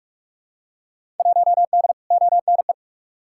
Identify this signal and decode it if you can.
Morse code, heard as a keyed tone, about 43 wpm, 700 Hz. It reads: 1D ONE
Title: Field Day Exchanges - 45wpm